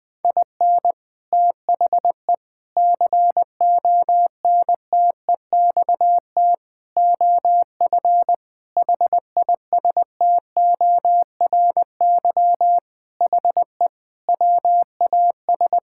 Morse code, heard as a keyed tone, about 20 wpm, 700 Hz. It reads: IN THE CONTEXT OF HISTORY HE WAS
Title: Sentences with 4-Word Phrases - 20wpm